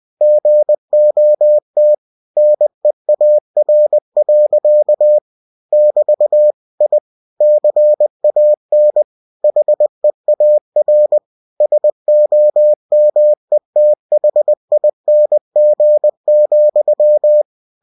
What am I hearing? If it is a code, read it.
Morse code, heard as a keyed tone, about 20 wpm, 600 Hz. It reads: GOT NEAR. <BT> I CAN HEAR SOMETHING,
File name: cw-books / winnie-the-pooh / winnie-20wpm-page-0043